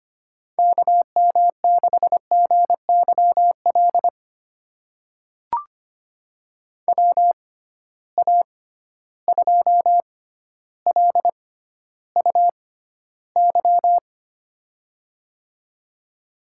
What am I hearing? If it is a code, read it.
Morse code, heard as a keyed tone, about 25 wpm, spaced as for 13 wpm, 700 Hz. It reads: KM6GYL WA2LUY
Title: USA Call Signs - 6 Characters - 6x Character Spacing - 25wpm